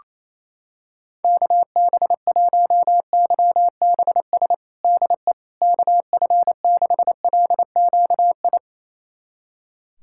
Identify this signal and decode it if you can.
Morse code, heard as a keyed tone, about 27 wpm, 700 Hz. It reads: KB1YBS DE KF6LQI
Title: USA Callsign DE Callsign - 6 Characters - 28wpm